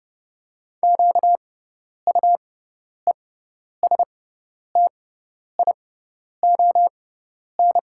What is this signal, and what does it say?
Morse code, heard as a keyed tone, about 30 wpm, spaced as for 10 wpm, 700 Hz. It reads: QUESTION
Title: 6x Character Spacing - Top 500 Words - 30wpm